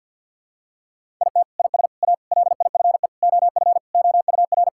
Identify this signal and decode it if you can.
Morse code, heard as a keyed tone, about 52 wpm, 700 Hz. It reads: IT IS A LIFE OF OUR
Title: Sentences from Top 1000 Words - Review Part 5 - 50wpm